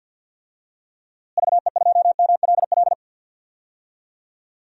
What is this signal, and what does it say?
Morse code, heard as a keyed tone, about 50 wpm, 700 Hz. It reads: VE2MLL
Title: DXCC-100 Call Signs - 6 Characters - Repeated 2x - 50wpm